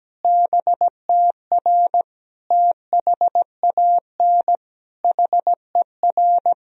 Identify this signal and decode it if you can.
Morse code, heard as a keyed tone, about 17 wpm, 700 Hz. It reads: BTR THAN HER
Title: Sentences from Top 400 Words and Abbreviations 17wpm